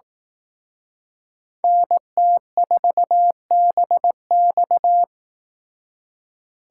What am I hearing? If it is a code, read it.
Morse code, heard as a keyed tone, about 18 wpm, 700 Hz. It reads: NT4BX
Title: Speed Racing - USA 2x2 Call Signs 15wpm